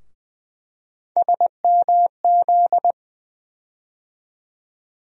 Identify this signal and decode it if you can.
Morse code, heard as a keyed tone, about 20 wpm, 700 Hz. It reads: SMZ
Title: USA Call Sign Postfixes - 3 Letters 20wpm